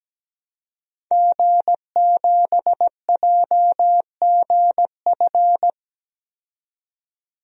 Morse code, heard as a keyed tone, about 17 wpm, 700 Hz. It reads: G7JGF
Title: DXCC-100 Call Signs - 5 Characters - 17wpm